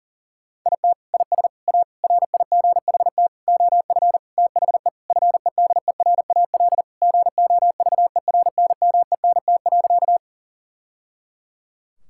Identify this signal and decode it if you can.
Morse code, heard as a keyed tone, about 40 wpm, 700 Hz. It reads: IT IS A RIGHT OF THE FEDERAL GOVERNMENT.
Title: Sentences from Top 1000 Words - Review Part 5 - 40wpm